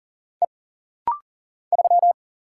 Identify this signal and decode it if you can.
Morse code, heard as a keyed tone, about 40 wpm, 700 Hz. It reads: E 3
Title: Single Letter-Number - Rapid-Fire 40wpm